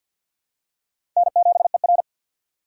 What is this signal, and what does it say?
Morse code, heard as a keyed tone, about 50 wpm, 700 Hz. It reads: N7ER
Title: USA Call Signs - 6x Character Spacing - 50wpm